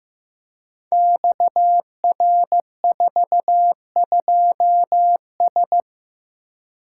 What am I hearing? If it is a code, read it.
Morse code, heard as a keyed tone, about 15 wpm, 700 Hz. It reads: XR42S
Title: Code Groups - 5 Letter:Numbers - 15wpm